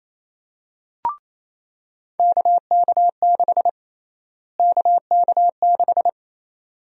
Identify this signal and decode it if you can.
Morse code, heard as a keyed tone, about 27 wpm, 700 Hz. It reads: KK6 KK6
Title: USA Call Sign Prefixes - Repeated 2x - 28wpm